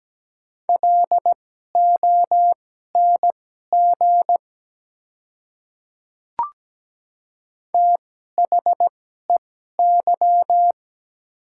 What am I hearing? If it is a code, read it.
Morse code, heard as a keyed tone, about 17 wpm, spaced as for 12 wpm, 700 Hz. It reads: LONG THEY
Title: 2x Character Spacing - Top 100 Words - 17wpm